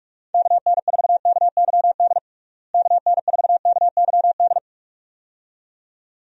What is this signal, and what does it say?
Morse code, heard as a keyed tone, about 46 wpm, 700 Hz. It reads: KN4KYD KN4KYD
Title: USA Call Signs - 6 Characters - Repeated 2x - 45wpm